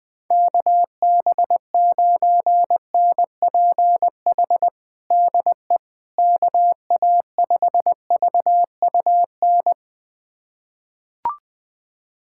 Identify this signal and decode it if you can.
Morse code, heard as a keyed tone, about 20 wpm, 700 Hz. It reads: KB9NPH DE KA5VUN
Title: USA Callsign DE Callsign 20wpm